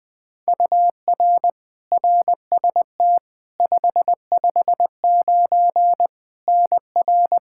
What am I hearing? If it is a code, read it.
Morse code, heard as a keyed tone, about 20 wpm, 700 Hz. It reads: UR RST 559 NR
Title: K1USN SST QSO Practice 20wpm